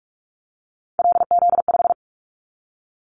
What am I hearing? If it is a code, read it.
Morse code, heard as a keyed tone, about 46 wpm, 700 Hz. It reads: LZ5
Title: Callsign Prefixes for Top 25 DXCC - 45wpm